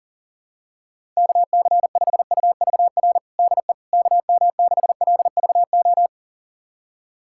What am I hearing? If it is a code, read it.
Morse code, heard as a keyed tone, about 40 wpm, 700 Hz. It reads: KC5UVR DE KM6LVO
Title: USA Callsign DE Callsign 40wpm